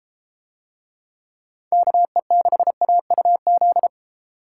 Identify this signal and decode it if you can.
Morse code, heard as a keyed tone, about 31 wpm, 700 Hz. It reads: KE6AUZ